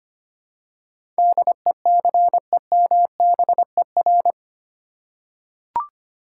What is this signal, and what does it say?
Morse code, heard as a keyed tone, about 25 wpm, 700 Hz. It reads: DECEMBER
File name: Eight Letter Words - 25wpm